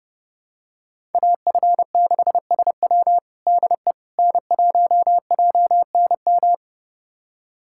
Morse code, heard as a keyed tone, about 32 wpm, 700 Hz. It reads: AF6SW DE N1JNM
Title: USA Callsign DE Callsign - 5 Characters - 30wpm